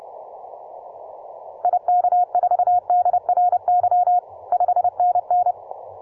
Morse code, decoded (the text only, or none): IK4DRY 5NN